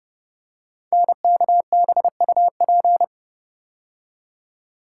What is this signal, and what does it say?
Morse code, heard as a keyed tone, about 30 wpm, 700 Hz. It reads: NKBUP
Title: Code Groups - 5 Letters - 30wpm